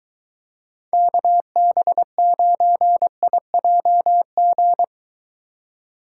Morse code, heard as a keyed tone, about 23 wpm, 700 Hz. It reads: KB9IJG